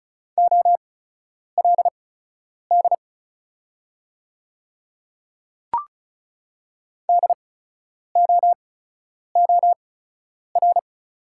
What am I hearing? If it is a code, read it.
Morse code, heard as a keyed tone, about 36 wpm, spaced as for 10 wpm, 690 Hz. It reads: OLD DOOR
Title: 8x Character Spacing - Top 500 Words - 35wpm